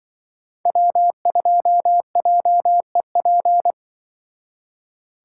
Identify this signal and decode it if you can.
Morse code, heard as a keyed tone, about 24 wpm, 700 Hz. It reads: W2JEP
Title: Speed Racing - USA 1x3 Call Signs 20wpm